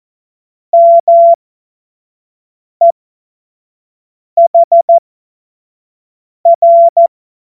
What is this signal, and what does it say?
Morse code, heard as a keyed tone, about 14 wpm, spaced as for 5 wpm, 680 Hz. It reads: MEHR
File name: Morsekurs / Witze / Witz 07. Dezember 2021, 25 F70